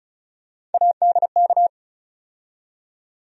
Morse code, heard as a keyed tone, about 36 wpm, 700 Hz. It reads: ADK
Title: USA Call Sign Postfixes - 3 Letters 35wpm